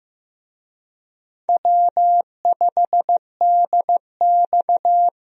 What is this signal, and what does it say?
Morse code, heard as a keyed tone, about 15 wpm, 700 Hz. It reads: W5DX
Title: USA Call Signs - 4 Characters - Repeated 2x - 15wpm